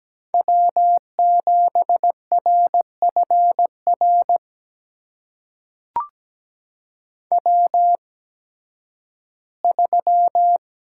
Morse code, heard as a keyed tone, about 17 wpm, spaced as for 11 wpm, 700 Hz. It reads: W7RFR W3
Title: USA Call Signs - 8x Character Spacing - 17wpm